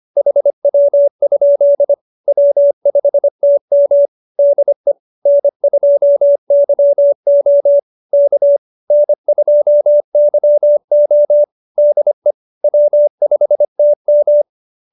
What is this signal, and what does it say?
Morse code, heard as a keyed tone, about 25 wpm, 570 Hz. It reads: HW? W5TM DE N2YO K N2YO DE W5TM